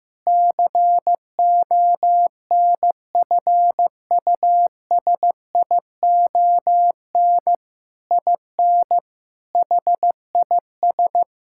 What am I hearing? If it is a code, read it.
Morse code, heard as a keyed tone, about 15 wpm, 700 Hz. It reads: CONFUSION IN HIS
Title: Sentences from Top 1600 Words 15wpm